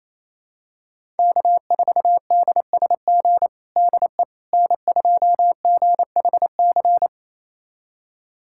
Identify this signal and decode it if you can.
Morse code, heard as a keyed tone, about 28 wpm, 700 Hz. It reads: K4DSG DE N2GHC
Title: USA Callsign DE Callsign - 5 Characters - 28wpm